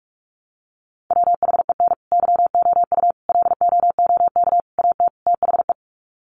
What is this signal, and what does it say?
Morse code, heard as a keyed tone, about 44 wpm, 700 Hz. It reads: WHEN YOU LOOK AT THE
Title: Sets of 5 Words 45wpm